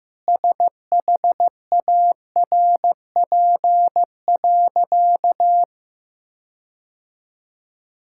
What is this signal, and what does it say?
Morse code, heard as a keyed tone, about 15 wpm, 700 Hz. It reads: SHARP.
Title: Sentences from Top 1600 Words 15wpm